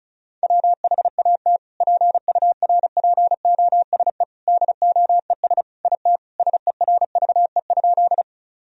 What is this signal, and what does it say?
Morse code, heard as a keyed tone, about 34 wpm, 700 Hz. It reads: WHAT PURPOSE DOES IT SERVE?
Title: Sentences from Top 1000 Words - Review Part 2 - 35wpm